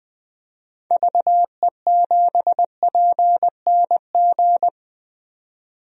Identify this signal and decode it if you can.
Morse code, heard as a keyed tone, about 20 wpm, 700 Hz. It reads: VE7PNG
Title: DXCC-100 Call Signs - 6 Characters - 20wpm